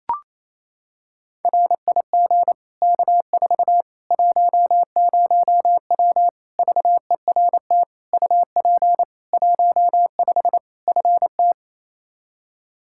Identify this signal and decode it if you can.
Morse code, heard as a keyed tone, about 28 wpm, 700 Hz. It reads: RIG K4 10W VERT UP 15 FT
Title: Practice QSO - Exchange 2 - Info - 28wpm